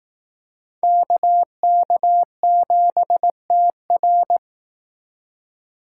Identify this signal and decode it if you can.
Morse code, heard as a keyed tone, about 18 wpm, 700 Hz. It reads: KK7TR
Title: Speed Racing - USA 2x2 Call Signs 15wpm